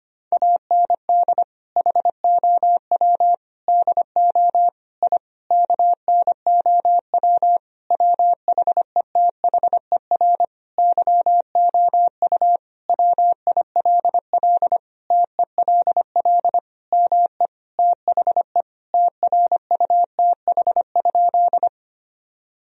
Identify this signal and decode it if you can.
Morse code, heard as a keyed tone, about 25 wpm, 700 Hz. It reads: AND HOW DO I KNOW WHETHER YOU WILL TELL ME THE TRUTH?
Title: Sentences from Top 1000 Words - Review Part 3 - 25wpm